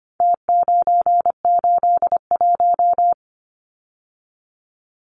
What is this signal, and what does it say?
Morse code, heard as a keyed tone, about 25 wpm, 700 Hz. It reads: T981
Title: Numbers - Contest Style - 4-Digit 25wpm